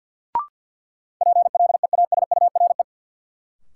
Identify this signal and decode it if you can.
Morse code, heard as a keyed tone, about 46 wpm, 700 Hz. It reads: PLEASURE
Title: Sentences from Top 1600 Words - New Word Drill - 50wpm